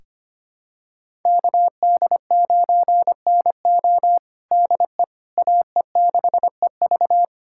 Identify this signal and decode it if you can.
Morse code, heard as a keyed tone, about 25 wpm, 700 Hz. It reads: KD9NO DE AE6EV
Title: USA Callsign DE Callsign - 5 Characters - 25wpm